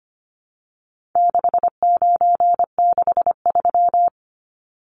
Morse code, heard as a keyed tone, about 25 wpm, 700 Hz. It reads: 6963
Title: Numbers - 4-Digit 25wpm